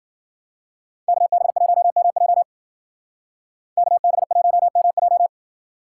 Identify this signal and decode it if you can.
Morse code, heard as a keyed tone, about 53 wpm, 700 Hz. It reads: KB1MJ KB1MJ